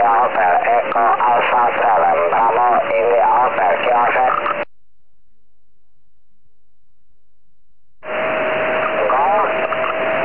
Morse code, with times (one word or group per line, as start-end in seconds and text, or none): none